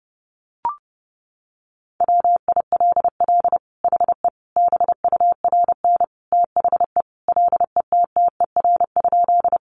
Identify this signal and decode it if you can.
Morse code, heard as a keyed tone, about 30 wpm, 700 Hz. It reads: WILL HE BURN THE LETTER?
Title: Sentences from Top 1300 Words 30wpm